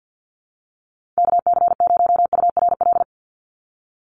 Morse code, heard as a keyed tone, about 50 wpm, 700 Hz. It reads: KC0URL
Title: Call Signs - 50wpm